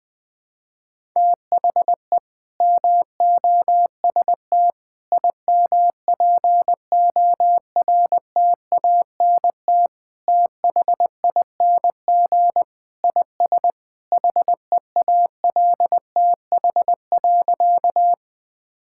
Morse code, heard as a keyed tone, about 21 wpm, 700 Hz. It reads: THE MOST IMPORTANT THING IS HEALTH.
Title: Sentences with 3-Word Phrases - 20wpm